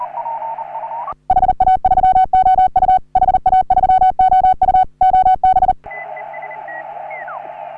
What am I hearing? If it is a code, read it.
Morse code, heard as a keyed tone, about 40 wpm, 750 Hz, with radio noise. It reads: HA3OU HA3OU OB
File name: CW / HA3OU RHOBCW REC